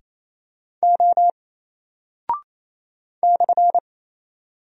Morse code, heard as a keyed tone, about 27 wpm, 700 Hz. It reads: O /
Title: Single Character (Letter:Number:Punctuation:Prosign) 28wpm